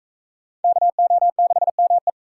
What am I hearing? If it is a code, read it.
Morse code, heard as a keyed tone, about 42 wpm, 700 Hz. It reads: KO6ME